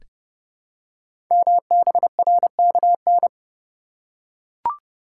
Morse code, heard as a keyed tone, about 30 wpm, 700 Hz. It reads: MBRKN
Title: Code Groups - 5 Letters - 30wpm